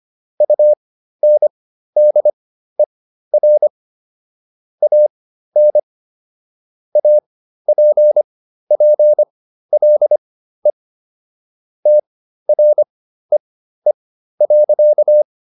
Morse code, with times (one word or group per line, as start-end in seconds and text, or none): 0.4-3.7 UNDER
4.8-5.8 AN
7.0-10.7 APPLE
11.9-15.2 TREE.